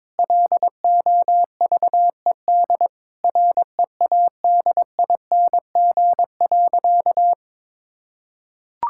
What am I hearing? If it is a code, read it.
Morse code, heard as a keyed tone, about 22 wpm, 700 Hz. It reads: LOVED READING.
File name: practice / 22wpm / Sentences with 8-Word Phrases - 22wpm